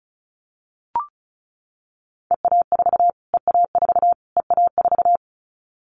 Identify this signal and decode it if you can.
Morse code, heard as a keyed tone, about 34 wpm, 700 Hz. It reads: EA4 EA4 EA4
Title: Callsign Prefixes for Top 25 DXCC - Repeated 3x - 35wpm